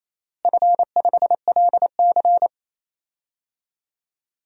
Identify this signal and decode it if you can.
Morse code, heard as a keyed tone, about 28 wpm, 700 Hz. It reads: F5LC
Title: DXCC-100 Call Signs - 4 Characters - Repeated 3x - 28wpm